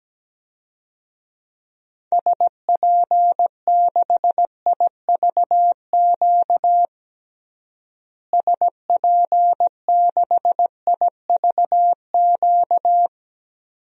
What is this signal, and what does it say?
Morse code, heard as a keyed tone, about 17 wpm, 700 Hz. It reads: SP6IVQ SP6IVQ
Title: DXCC-100 Call Signs - 6 Characters - Repeated 2x - 17wpm